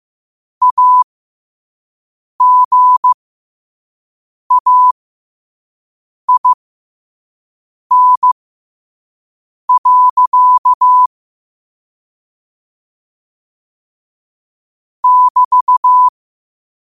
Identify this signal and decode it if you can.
Morse code, heard as a keyed tone, about 15 wpm, spaced as for 5 wpm, 1000 Hz. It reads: AGAIN. <BT>
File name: qotdfiles / 5wpm